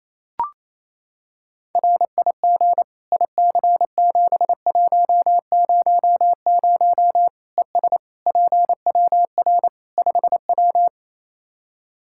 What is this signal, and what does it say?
Morse code, heard as a keyed tone, about 28 wpm, 700 Hz. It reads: RIG IC7100 ES PWR 5W
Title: Practice QSO - Exchange 2 - Info - 28wpm